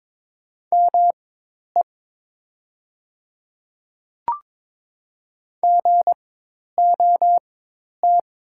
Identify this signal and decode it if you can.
Morse code, heard as a keyed tone, about 22 wpm, spaced as for 10 wpm, 700 Hz. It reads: ME GOT
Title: 4x Character Spacing - Top 300 Words - 22wpm